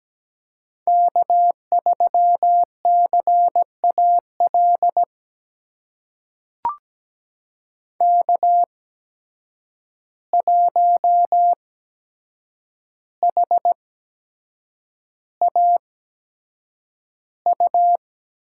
Morse code, heard as a keyed tone, about 17 wpm, spaced as for 7 wpm, 700 Hz. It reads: K3CAL K1HAU
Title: USA Call Signs - 5 Characters - 8x Character Spacing - 17wpm